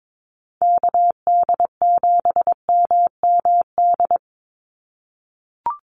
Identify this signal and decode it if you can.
Morse code, heard as a keyed tone, about 22 wpm, 700 Hz. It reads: KD7MMD
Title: USA Call Signs 22wpm